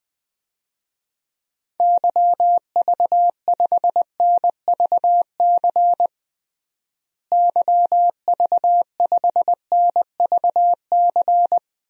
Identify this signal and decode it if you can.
Morse code, heard as a keyed tone, about 20 wpm, 700 Hz. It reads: YV5NVC YV5NVC
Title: DXCC-100 Call Signs - 6 Characters - Repeated 3x - 20wpm